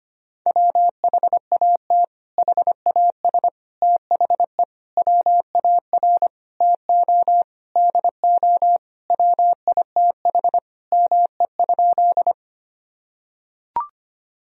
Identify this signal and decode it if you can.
Morse code, heard as a keyed tone, about 25 wpm, 700 Hz. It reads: WHAT HAS THE WAR TO DO WITH ME?
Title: Sentences from Top 1000 Words - Review Part 2 - 25wpm